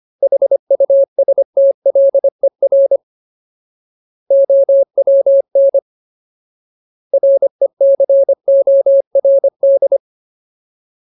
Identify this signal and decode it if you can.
Morse code, heard as a keyed tone, about 25 wpm, 550 Hz. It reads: HUSTLER OWN RECORD
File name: Ultimate Word List / 25WPM Ultimate